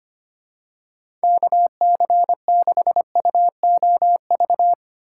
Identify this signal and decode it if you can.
Morse code, heard as a keyed tone, about 25 wpm, 700 Hz. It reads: KC6UOV
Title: USA Call Signs - 3x Character Spacing - 25wpm